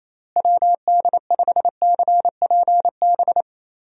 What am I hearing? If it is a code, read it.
Morse code, heard as a keyed tone, about 28 wpm, 700 Hz. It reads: WD5CPB
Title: USA Call Signs - 6 Characters - Repeated 2x - 28wpm